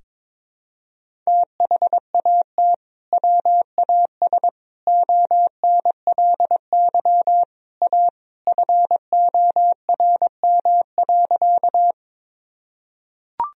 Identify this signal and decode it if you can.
Morse code, heard as a keyed tone, about 21 wpm, 700 Hz. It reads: THAT WAS ONLY A FORM.
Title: Sentences from Top 1000 Words - Review Part 1 - 22wpm